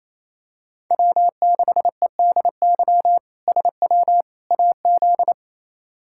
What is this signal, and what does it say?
Morse code, heard as a keyed tone, about 27 wpm, 700 Hz. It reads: W6EDY SW AZ
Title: K1USN SST QSO Practice - 28wpm